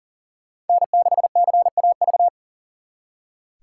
Morse code, heard as a keyed tone, about 52 wpm, 700 Hz. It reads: N6CAV